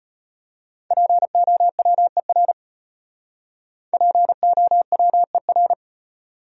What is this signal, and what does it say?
Morse code, heard as a keyed tone, about 32 wpm, 700 Hz. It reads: POWER POWER